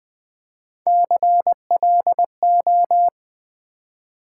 Morse code, heard as a keyed tone, about 20 wpm, 700 Hz. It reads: CLO
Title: Build A Word - 3 Letters At A Time - 8-Letter Words - 20wpm